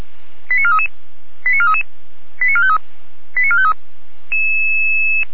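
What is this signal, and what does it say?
Five-tone selective call (ZVEI), heard as quick runs of five tones, tones of about 80 ms. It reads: 89433 89433 89453 89453
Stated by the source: Five-tone ZVEI selective call received on an SDR, from audio.f85360.feuerwehr-orlamuende.org, a fire brigade (Feuerwehr Orlamünde) server.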